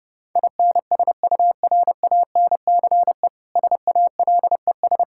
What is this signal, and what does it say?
Morse code, heard as a keyed tone, about 29 wpm, 700 Hz. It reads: INSURANCE SALES
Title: Practice QSO - Exchange 3 - More Info - 30wpm